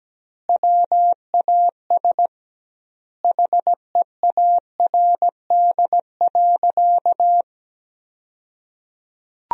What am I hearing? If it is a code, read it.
Morse code, heard as a keyed tone, about 17 wpm, 700 Hz. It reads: WAS HEARD.
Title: 2x Word-Spacing - Sentences from Top 400 Words - 17wpm